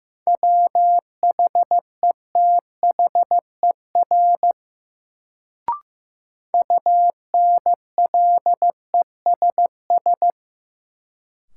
Morse code, heard as a keyed tone, about 16 wpm, 700 Hz. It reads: WHETHER UNLESS
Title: Sentences from Top 1300 Words - New Word Drill Alt - 15wpm